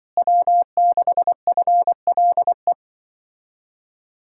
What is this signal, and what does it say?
Morse code, heard as a keyed tone, about 24 wpm, 700 Hz. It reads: W6FLE
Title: Speed-Racing Common QSO Phrases - 20wpm